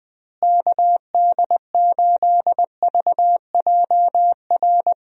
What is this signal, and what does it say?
Morse code, heard as a keyed tone, about 20 wpm, 700 Hz. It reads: KD8VJR
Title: USA Call Signs - 6 Characters - Repeated 2x - 20wpm